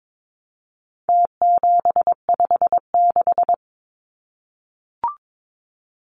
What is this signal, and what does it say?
Morse code, heard as a keyed tone, about 22 wpm, 700 Hz. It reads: T756
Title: Numbers - Contest Style - 4-Digit 22wpm